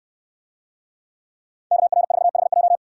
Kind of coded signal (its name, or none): Morse code